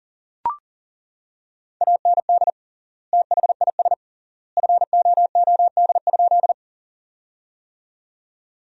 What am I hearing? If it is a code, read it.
Morse code, heard as a keyed tone, about 40 wpm, 700 Hz. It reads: AND THIS FOOD?
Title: 3x Word-Spacing - Sentences from Top 300 Words - 40wpm